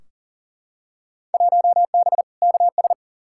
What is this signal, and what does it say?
Morse code, heard as a keyed tone, about 40 wpm, 700 Hz. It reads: 1B KS
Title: Field Day Exchanges Encore - 40wpm